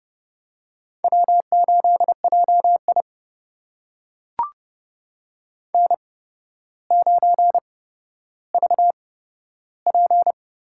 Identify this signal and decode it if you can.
Morse code, heard as a keyed tone, about 30 wpm, spaced as for 13 wpm, 700 Hz. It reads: W8JI N9VP
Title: USA Call Signs - 4 Characters - 8x Character Spacing - 30wpm